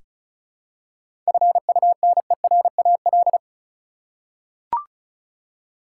Morse code, heard as a keyed tone, about 36 wpm, 700 Hz. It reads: FUNERAL
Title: Seven Letter Words - 35wpm